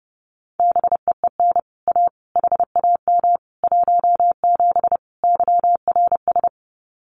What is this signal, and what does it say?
Morse code, heard as a keyed tone, about 30 wpm, 700 Hz. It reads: BEEN A HAM 17 YRS
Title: Common QSO Phrases Encore 30wpm